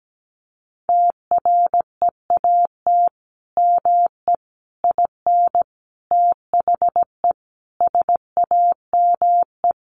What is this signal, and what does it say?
Morse code, heard as a keyed tone, about 17 wpm, 700 Hz. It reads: TREAT ME IN THE SAME
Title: Sentences from Top 1300 Words 17wpm